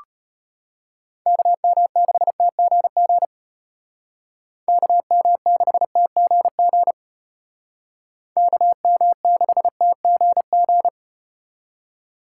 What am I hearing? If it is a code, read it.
Morse code, heard as a keyed tone, about 33 wpm, 700 Hz. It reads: KM6TGG KM6TGG KM6TGG